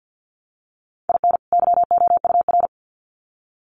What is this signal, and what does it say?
Morse code, heard as a keyed tone, about 50 wpm, 700 Hz. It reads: IN YOUR
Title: Sets of 2 Words 50wpm